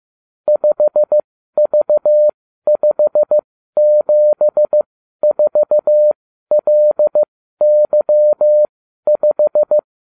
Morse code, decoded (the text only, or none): 5V574LY5